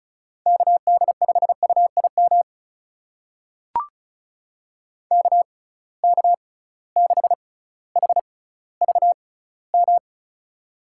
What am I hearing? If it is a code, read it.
Morse code, heard as a keyed tone, about 36 wpm, spaced as for 18 wpm, 700 Hz. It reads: KD5UIM KK6HVM
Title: USA Call Signs - 6x Character Spacing - 35wpm